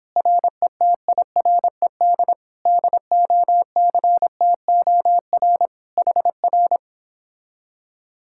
Morse code, heard as a keyed tone, about 29 wpm, 700 Hz. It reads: RETIRED DOCTOR HR